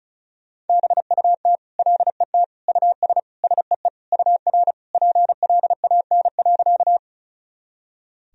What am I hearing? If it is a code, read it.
Morse code, heard as a keyed tone, about 34 wpm, 700 Hz. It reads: BUT LET US SEE UR PLAN.
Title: Sentences from Top 500 Words and Abbreviations 35wpm